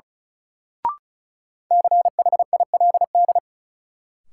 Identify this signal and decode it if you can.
Morse code, heard as a keyed tone, about 36 wpm, 700 Hz. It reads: CHILD
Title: Sentences from Top 800 Words - New Word Drill - 35wpm